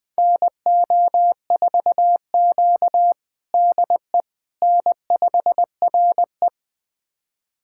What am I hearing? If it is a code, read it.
Morse code, heard as a keyed tone, about 21 wpm, 700 Hz. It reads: NO4Q DE N5RE
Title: USA Callsign DE Callsign - 4 Characters - 20wpm